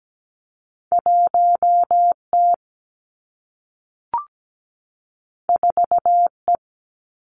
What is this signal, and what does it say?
Morse code, heard as a keyed tone, about 17 wpm, 700 Hz. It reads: 1 T 4 E
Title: Two Challenging Letter-Number Combinations 17wpm